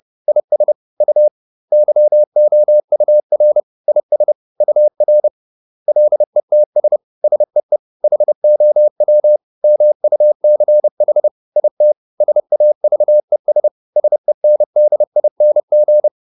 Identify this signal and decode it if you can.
Morse code, heard as a keyed tone, about 30 wpm, 600 Hz. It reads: IS U YOUR IS UR LETS SEE HOW MUCH IT SAVES SENDING